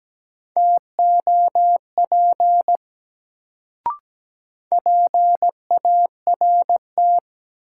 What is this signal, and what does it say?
Morse code, heard as a keyed tone, about 17 wpm, 700 Hz. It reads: TOP PART
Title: Sets of 2 Words - New Word Drill - 17wpm